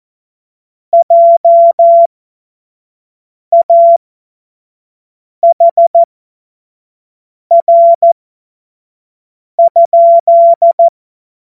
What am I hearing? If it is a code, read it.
Morse code, heard as a keyed tone, about 14 wpm, spaced as for 5 wpm, 680 Hz. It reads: JAHR?